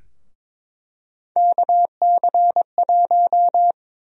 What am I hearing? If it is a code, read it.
Morse code, heard as a keyed tone, about 23 wpm, 700 Hz. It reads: KC1
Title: USA Call Sign Prefixes - 22wpm